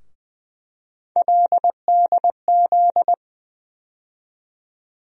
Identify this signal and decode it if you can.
Morse code, heard as a keyed tone, about 21 wpm, 700 Hz. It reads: LDZ
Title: USA Call Sign Postfixes - 3 Letters 20wpm